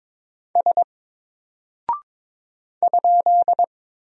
Morse code, heard as a keyed tone, about 22 wpm, 700 Hz. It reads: S ?
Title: Single Character (Letter:Number:Punctuation:Prosign) 22wpm